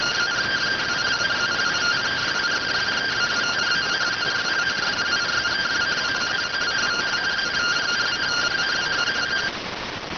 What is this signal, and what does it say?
Two-tone teleprinter signal, about 170 Hz shift.